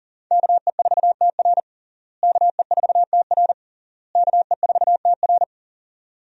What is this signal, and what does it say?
Morse code, heard as a keyed tone, about 40 wpm, 700 Hz. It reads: KE4TR KE4TR KE4TR
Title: USA Call Signs - 5 Characters - Repeated 3x - 40wpm